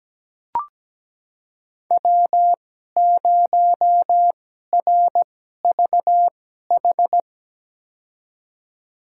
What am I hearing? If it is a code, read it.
Morse code, heard as a keyed tone, about 17 wpm, spaced as for 12 wpm, 700 Hz. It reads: W0RVH
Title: USA Call Signs - 2x Character Spacing - 17wpm